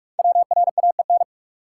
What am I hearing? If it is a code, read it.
Morse code, heard as a keyed tone, about 44 wpm, 700 Hz. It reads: WRAEN
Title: Code Groups - 5 Letters - 45wpm